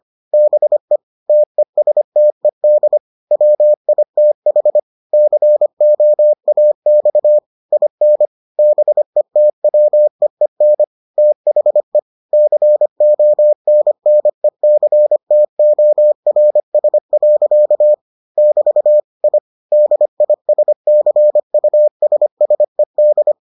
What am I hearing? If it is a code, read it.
Morse code, heard as a keyed tone, about 25 wpm, 600 Hz. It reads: BE TESTED WITH COAX IN BETWEEN THE CONNECTORS. <BT> I DISCUSSED